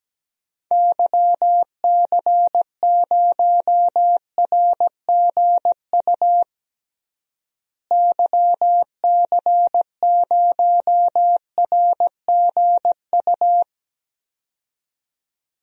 Morse code, heard as a keyed tone, about 17 wpm, 700 Hz. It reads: YC0RGU YC0RGU
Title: DXCC-100 Call Signs - 6 Characters - Repeated 2x - 17wpm